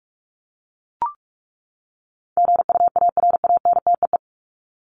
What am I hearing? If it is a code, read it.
Morse code, heard as a keyed tone, about 44 wpm, 700 Hz. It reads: GUARANTEE
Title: Top 1501-1750 Words 45wpm